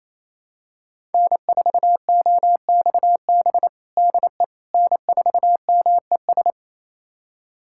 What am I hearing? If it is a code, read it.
Morse code, heard as a keyed tone, about 28 wpm, 700 Hz. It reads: N4OXB DE N4MES
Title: USA Callsign DE Callsign - 5 Characters - 28wpm